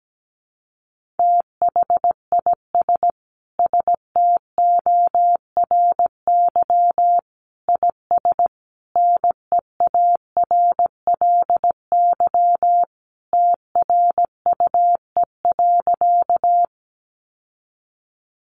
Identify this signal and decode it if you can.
Morse code, heard as a keyed tone, about 17 wpm, 700 Hz. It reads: THIS STORY IS NEARLY TRUE.
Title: Sentences from Top 1300 Words 17wpm